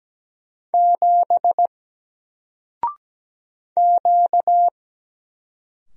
Morse code, heard as a keyed tone, about 17 wpm, 700 Hz. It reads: 7 Q
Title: Single Letter-Number 17wpm